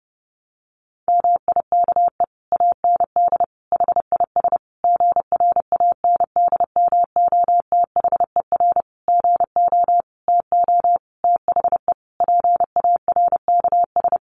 Morse code, heard as a keyed tone, about 30 wpm, 700 Hz. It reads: MIKE AND HIS GRANDMOTHER GO TO THE PARKS
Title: Additional Easy Phrases 30wpm